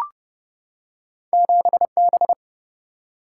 Morse code, heard as a keyed tone, about 30 wpm, 700 Hz. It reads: 7B
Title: Single-Letter Single-Number 30wpm